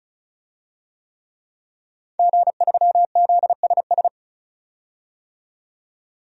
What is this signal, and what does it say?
Morse code, heard as a keyed tone, about 36 wpm, 700 Hz. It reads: G3ZSS
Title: DXCC-100 Call Signs - 5 Characters - 35wpm